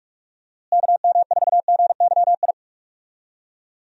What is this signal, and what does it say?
Morse code, heard as a keyed tone, about 44 wpm, 700 Hz. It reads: KM4GYI
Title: USA Call Signs - 6 Characters - Repeated 2x - 45wpm